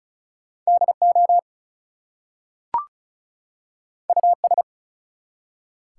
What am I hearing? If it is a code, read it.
Morse code, heard as a keyed tone, about 35 wpm, 700 Hz. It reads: DO US
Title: Two Letter Words - 35wpm